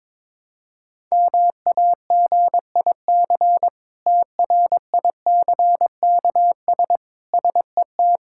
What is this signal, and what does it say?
Morse code, heard as a keyed tone, about 21 wpm, 700 Hz. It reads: MAGIC TRICKS SET
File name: Difficult Words with Hints - 22wpm